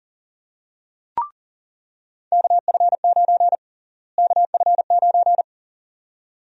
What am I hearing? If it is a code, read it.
Morse code, heard as a keyed tone, about 40 wpm, 700 Hz. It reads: KF9 KF9
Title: USA Call Sign Prefixes - Repeated 2x - 40wpm